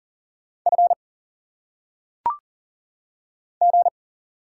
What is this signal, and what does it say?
Morse code, heard as a keyed tone, about 38 wpm, 700 Hz. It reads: F G
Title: Single Letters - Phonetic Pronunciation 40wpm